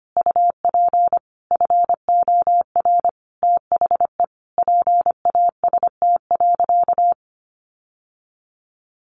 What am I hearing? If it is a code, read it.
Morse code, heard as a keyed tone, about 25 wpm, 700 Hz. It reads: UP FOR THE PAST.
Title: Sentences from Top 1100 Words 25wpm